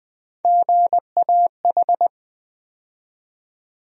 Morse code, heard as a keyed tone, about 20 wpm, 700 Hz. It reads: GAH
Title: USA Call Sign Postfixes - 3 Letters 20wpm